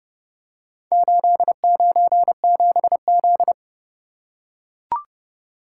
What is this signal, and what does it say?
Morse code, heard as a keyed tone, about 29 wpm, 700 Hz. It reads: 897Z